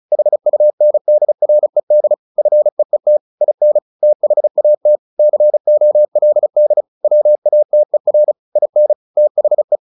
Morse code, heard as a keyed tone, about 34 wpm, 600 Hz. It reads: HUNDRED FEET IN THAT COLD WATER IN THE
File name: the old man and the sea / the old man and the sea 1 35wpm